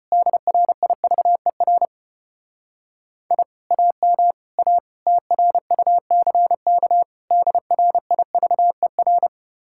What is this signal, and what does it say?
Morse code, heard as a keyed tone, about 32 wpm, 700 Hz. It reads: DRIVER I AM A TRUCK DRIVER